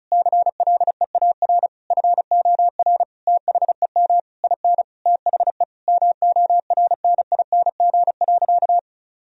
Morse code, heard as a keyed tone, about 36 wpm, 700 Hz. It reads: CLEAR FOR THEM IN THE MORNING.
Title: Sentences from Top 1000 Words - Review Part 2 - 35wpm